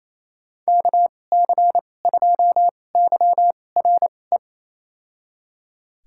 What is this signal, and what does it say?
Morse code, heard as a keyed tone, about 28 wpm, spaced as for 20 wpm, 700 Hz. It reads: KC2YRE
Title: USA Call Signs - 6 Characters - 2x Character Spacing - 28wpm